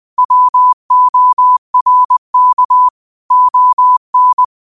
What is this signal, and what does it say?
Morse code, heard as a keyed tone, about 20 wpm, 1000 Hz. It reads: WORK ON
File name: qotdfiles / 20wpm